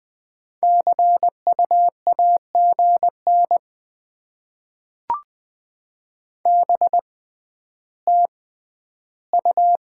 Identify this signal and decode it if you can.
Morse code, heard as a keyed tone, about 21 wpm, spaced as for 12 wpm, 700 Hz. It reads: CU AGN BTU
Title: Top 20 QSO Elements - 6x Character Spacing - 20wpm